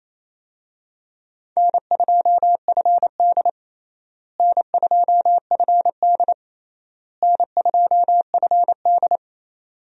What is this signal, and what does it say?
Morse code, heard as a keyed tone, about 28 wpm, 700 Hz. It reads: N2FD N2FD N2FD
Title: USA Call Signs - 4 Characters - Repeated 3x - 28wpm